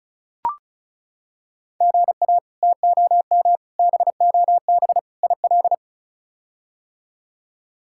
Morse code, heard as a keyed tone, about 36 wpm, 700 Hz. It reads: GA TOM BOB IL
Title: K1USN SST QSO Practice - 35wpm